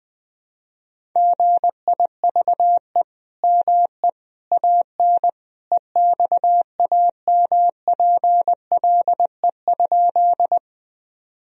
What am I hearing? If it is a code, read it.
Morse code, heard as a keyed tone, about 20 wpm, 700 Hz. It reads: GIVE ME AN EXAMPLE?
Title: Sentences from Top 1000 Words - Review Part 1 - 20wpm